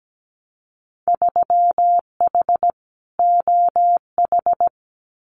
Morse code, heard as a keyed tone, about 17 wpm, 700 Hz. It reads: 3H OH
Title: Winter Field Day Exchanges Encore 17wpm